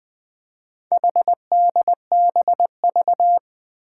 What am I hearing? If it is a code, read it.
Morse code, heard as a keyed tone, about 20 wpm, 700 Hz. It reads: HDBV
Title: Four Challenging Letter-Number Combinations 20wpm